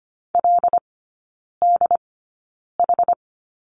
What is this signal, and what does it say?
Morse code, heard as a keyed tone, about 25 wpm, spaced as for 9 wpm, 700 Hz. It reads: LDH
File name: Lesson 17e - Letter u - Three Characters 25wpm